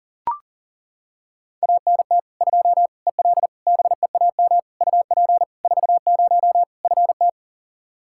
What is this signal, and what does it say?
Morse code, heard as a keyed tone, about 40 wpm, 700 Hz. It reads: ANT 2 EL BEAM UP 40 FT
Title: Practice QSO - Exchange 2 - Info - 40wpm